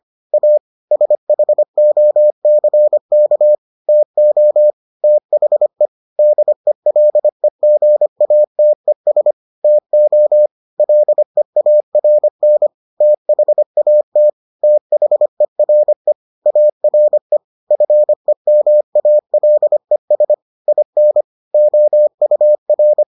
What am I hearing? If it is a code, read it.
Morse code, heard as a keyed tone, about 25 wpm, 600 Hz. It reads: A SHOCK TO THE DELEGATES TO LEARN THAT THERE ARE FEMALES IN OUR